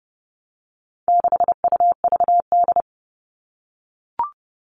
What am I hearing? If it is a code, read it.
Morse code, heard as a keyed tone, about 30 wpm, 700 Hz. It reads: 6UVD
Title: All Cut Numbers - 4-Digit 30wpm